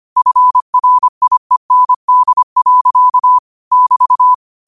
Morse code, heard as a keyed tone, about 25 wpm, 1000 Hz. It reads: FRIEND. <BT>
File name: qotdfiles / 25wpm